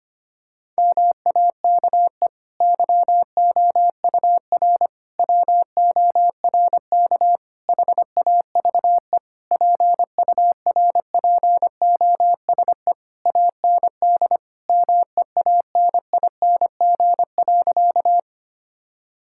Morse code, heard as a keyed tone, about 25 wpm, 700 Hz. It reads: MAKE YOUR WORK HAVE PURPOSE AND MEANING.
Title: Sentences from Top 1000 Words - Review Part 6 - 25wpm